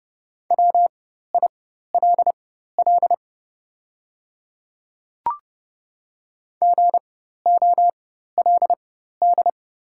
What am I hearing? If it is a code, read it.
Morse code, heard as a keyed tone, about 30 wpm, spaced as for 14 wpm, 690 Hz. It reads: WILL GOLD
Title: 4x Character Spacing - Top 500 Words - 30wpm